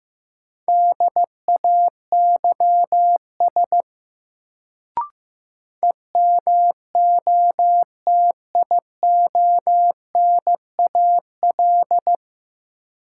Morse code, heard as a keyed tone, about 15 wpm, 700 Hz. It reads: DAYS EMOTIONAL
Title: Sentences from Top 1000 Words - Review Part 5 - New Word Drill - 15wpm